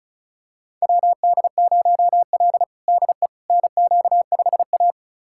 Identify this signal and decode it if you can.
Morse code, heard as a keyed tone, about 36 wpm, 700 Hz. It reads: WD0L DE NQ5A
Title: USA Callsign DE Callsign - 4 Characters - 35wpm